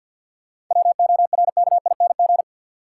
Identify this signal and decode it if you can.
Morse code, heard as a keyed tone, about 54 wpm, 700 Hz. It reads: WORKING
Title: Sentences from Top 700 Words - New Word Drill - 50wpm